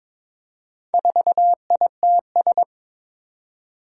Morse code, heard as a keyed tone, about 22 wpm, 700 Hz. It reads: 4ITS
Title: Four Challenging Letter-Number Combinations 22wpm